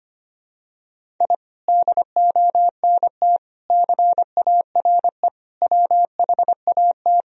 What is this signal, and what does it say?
Morse code, heard as a keyed tone, about 25 wpm, 700 Hz. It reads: I DONT CARE WHAT
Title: Sentences from Top 1000 Words - Review Part 1 - 25wpm